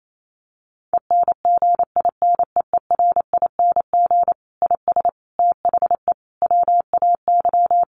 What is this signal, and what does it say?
Morse code, heard as a keyed tone, about 28 wpm, 700 Hz. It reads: ENGINEERING IS THE WAY
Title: Sentences from Top 1500 Words 28wpm